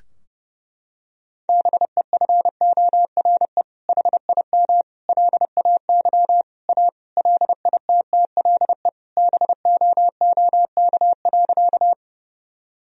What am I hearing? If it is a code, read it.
Morse code, heard as a keyed tone, about 30 wpm, 700 Hz. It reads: BEFORE HIM LAY A LITTLE BOOK.
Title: Sentences from Top 1000 Words - Review Part 2 - 30wpm